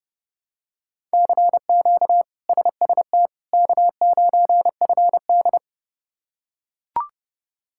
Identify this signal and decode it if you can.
Morse code, heard as a keyed tone, about 32 wpm, 700 Hz. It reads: CQ SST K9FD
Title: K1USN SST QSO Practice - 30wpm